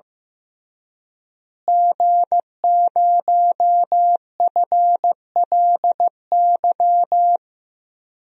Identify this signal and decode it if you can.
Morse code, heard as a keyed tone, about 15 wpm, 700 Hz. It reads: G0FLY
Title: DXCC-100 Call Signs - 5 Characters - Repeated 3x - 15wpm